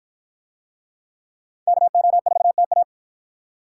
Morse code, heard as a keyed tone, about 53 wpm, 700 Hz. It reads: KO4TA